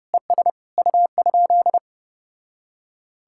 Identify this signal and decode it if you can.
Morse code, heard as a keyed tone, about 33 wpm, 700 Hz. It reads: ES U?